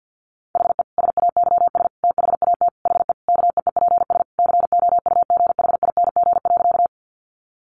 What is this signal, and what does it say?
Morse code, heard as a keyed tone, about 50 wpm, 700 Hz. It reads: HE SAYS THAT HE KEEPS COUGHING.
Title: Additional Easy Phrases 50wpm